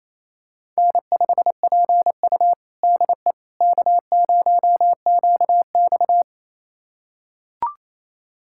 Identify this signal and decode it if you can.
Morse code, heard as a keyed tone, about 28 wpm, 700 Hz. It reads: N5PU DE K0QX
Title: USA Callsign DE Callsign - 4 Characters - 28wpm